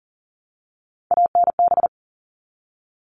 Morse code, heard as a keyed tone, about 40 wpm, 700 Hz. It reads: ANB
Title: All Cut Numbers - 3-Digit 40wpm